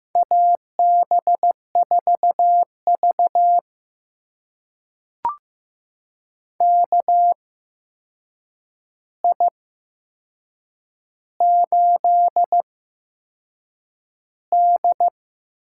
Morse code, heard as a keyed tone, about 15 wpm, spaced as for 6 wpm, 700 Hz. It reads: AB4V KI8D
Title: USA Call Signs - 4 Characters - 8x Character Spacing - 15wpm